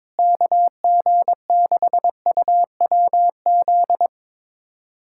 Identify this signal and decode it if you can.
Morse code, heard as a keyed tone, about 21 wpm, 700 Hz. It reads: KG6UWZ
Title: USA Callsign DE Callsign 22wpm